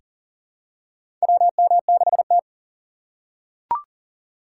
Morse code, heard as a keyed tone, about 40 wpm, 700 Hz. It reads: WM6T
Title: USA Call Signs - 4 Characters - 4x Character Spacing - 40wpm